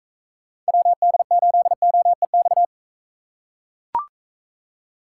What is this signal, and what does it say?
Morse code, heard as a keyed tone, about 43 wpm, 700 Hz. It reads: WD8OEX